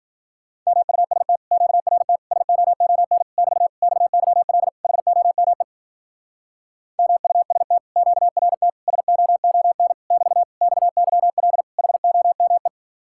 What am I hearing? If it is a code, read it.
Morse code, heard as a keyed tone, about 48 wpm, 700 Hz. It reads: MUST QRT SOON <BT> XYL HOME MUST QRT SOON <BT> XYL HOME
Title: Speed-Racing Common QSO Phrases - 40wpm